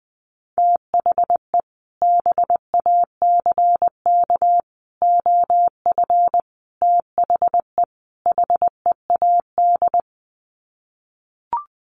Morse code, heard as a keyed tone, about 20 wpm, 700 Hz. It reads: THE BACK OF THE HEAD
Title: Sets of 5 Words Encore2 20wpm